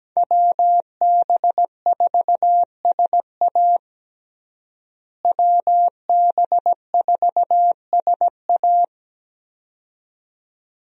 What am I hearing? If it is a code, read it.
Morse code, heard as a keyed tone, about 17 wpm, 700 Hz. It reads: WB4SA WB4SA
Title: USA Call Signs - 5 Characters - Repeated 2x - 17wpm